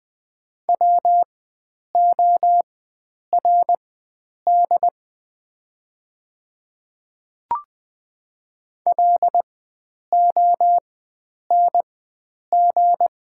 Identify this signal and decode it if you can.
Morse code, heard as a keyed tone, about 20 wpm, spaced as for 9 wpm, 690 Hz. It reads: WORD LONG
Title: 4x Character Spacing - Top 100 Words - 20wpm